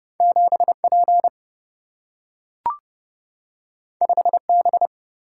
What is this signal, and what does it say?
Morse code, heard as a keyed tone, about 30 wpm, 700 Hz. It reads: 7 P 5 B
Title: Single-Letter Single-Number 30wpm